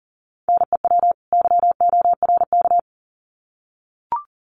Morse code, heard as a keyed tone, about 40 wpm, 700 Hz. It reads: NEW YORK
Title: Sets of 2 Words 40wpm